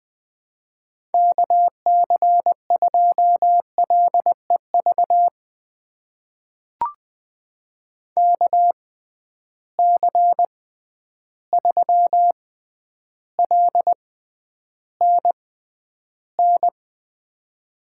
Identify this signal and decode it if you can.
Morse code, heard as a keyed tone, about 20 wpm, spaced as for 10 wpm, 700 Hz. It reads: KC2LEV KC3LNN
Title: USA Call Signs - 6 Characters - 6x Character Spacing - 20wpm